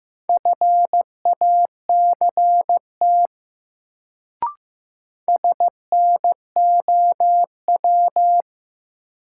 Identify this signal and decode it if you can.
Morse code, heard as a keyed tone, about 15 wpm, 700 Hz. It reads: FACT SNOW
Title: Sentences from Top 500 Words - New Word Drill - 15wpm